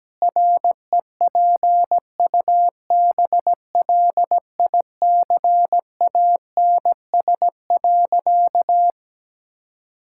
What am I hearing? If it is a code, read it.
Morse code, heard as a keyed tone, about 17 wpm, 700 Hz. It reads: REPUBLICANS.
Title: Pangram Challenge v2 - 17wpm